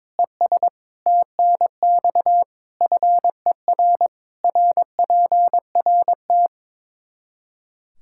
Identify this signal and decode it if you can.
Morse code, heard as a keyed tone, about 22 wpm, 700 Hz. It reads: ES TNX FER RPRT
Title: Practice QSO - Exchange 2 - Info - 22wpm